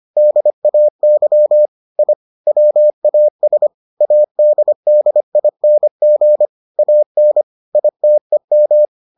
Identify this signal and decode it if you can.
Morse code, heard as a keyed tone, about 25 wpm, 600 Hz. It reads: DAY I WAS ADDING AN ITEM